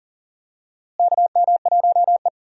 Morse code, heard as a keyed tone, about 39 wpm, 700 Hz. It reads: KM1E
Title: USA Call Signs - 4 Characters - Repeated 2x - 40wpm